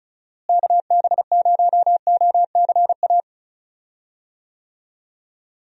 Morse code, heard as a keyed tone, about 34 wpm, 700 Hz. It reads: KB0OCA
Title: USA Call Signs - 6 Characters - Repeated 2x - 35wpm